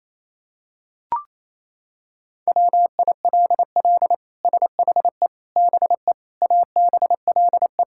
Morse code, heard as a keyed tone, about 28 wpm, 700 Hz. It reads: WILL SHE BE ABLE
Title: Sentences from Top 1000 Words - Review Part 6 - 28wpm